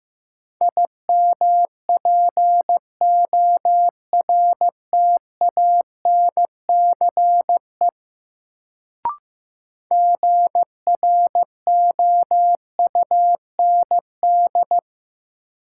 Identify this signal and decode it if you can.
Morse code, heard as a keyed tone, about 15 wpm, 700 Hz. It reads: IMPORTANCE GROUND
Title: Sets of 3 Words - New Word Drill - 15wpm